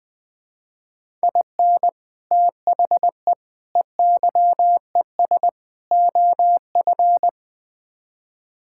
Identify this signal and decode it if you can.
Morse code, heard as a keyed tone, about 20 wpm, 700 Hz. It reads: IN THE EYES OF
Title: Build A Phrase - 5-Word Phrases - 20wpm